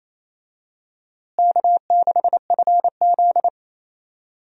Morse code, heard as a keyed tone, about 28 wpm, 700 Hz. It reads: K6FZ
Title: USA Call Signs - 4 Characters - Repeated 2x - 28wpm